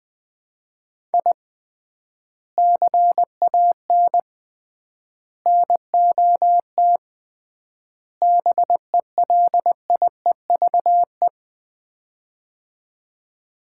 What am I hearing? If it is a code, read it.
Morse code, heard as a keyed tone, about 20 wpm, 700 Hz. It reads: I CAN NOT BELIEVE
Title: 3x Word-Spacing - Sets of 4 Words - 20wpm